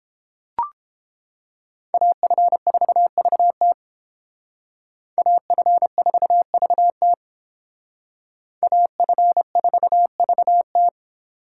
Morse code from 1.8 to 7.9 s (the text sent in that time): AF4VT AF4VT